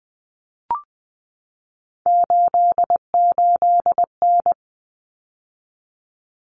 Morse code, heard as a keyed tone, about 20 wpm, 700 Hz. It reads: 88N
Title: Common Cut Numbers - 3-Digit 20wpm